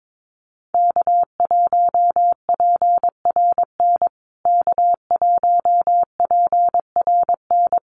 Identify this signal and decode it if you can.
Morse code, heard as a keyed tone, about 22 wpm, 700 Hz. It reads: K1PRN K1PRN
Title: USA Call Signs Repeated 2x 22wpm